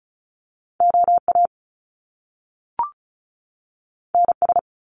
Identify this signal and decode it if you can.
Morse code, heard as a keyed tone, about 36 wpm, 700 Hz. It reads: OA NS
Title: Top Two Letter Combinations in Words 35wpm